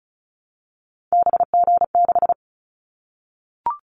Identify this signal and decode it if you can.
Morse code, heard as a keyed tone, about 36 wpm, 700 Hz. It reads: BG6
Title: Callsign Prefixes for Top 100 DXCC - 35wpm